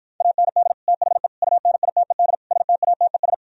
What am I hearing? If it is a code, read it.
Morse code, heard as a keyed tone, about 52 wpm, 700 Hz. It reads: AND THE UNITED STATES